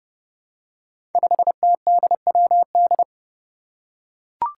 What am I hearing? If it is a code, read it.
Morse code, heard as a keyed tone, about 32 wpm, 700 Hz. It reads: 5TDWD
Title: Five Challenging Letter-Number Combinations 30wpm